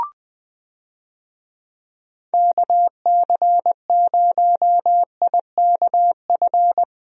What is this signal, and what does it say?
Morse code, heard as a keyed tone, about 20 wpm, 700 Hz. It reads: KC0IKF
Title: USA Call Signs - 6 Characters - Repeated 3x - 20wpm